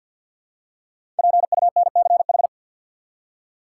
Morse code, heard as a keyed tone, about 49 wpm, 700 Hz. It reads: PUNCH
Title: Five Letter Words - 50wpm